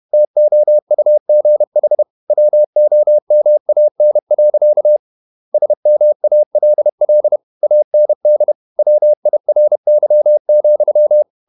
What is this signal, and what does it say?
Morse code, heard as a keyed tone, about 31 wpm, 600 Hz. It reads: TOUGH WOMAN. SMALL AND WIRY,
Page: Never 31wpm | CW med Gnister